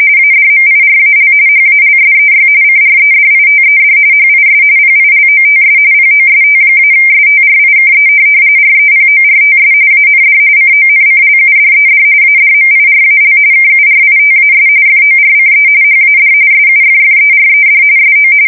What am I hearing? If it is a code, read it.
RTTY, about 45.45 baud. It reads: ADMIRAL HALSEY CREDITED HIS PILOTS POST-STRIKE REPORTS AND CONCLUDED ADMIRAL KURITAS FORCE HAD SUFFERED SO
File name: fsk170 / USS-New-Jersey